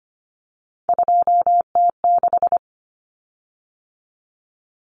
Morse code, heard as a keyed tone, about 25 wpm, 700 Hz. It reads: 2T6
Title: Common Cut Numbers - 3-Digit 25wpm